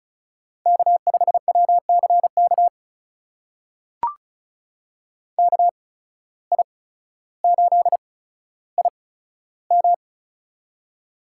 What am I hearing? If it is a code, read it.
Morse code, heard as a keyed tone, about 36 wpm, spaced as for 15 wpm, 700 Hz. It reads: K5WCK KI8IM
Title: USA Call Signs - 5 Characters - 8x Character Spacing - 35wpm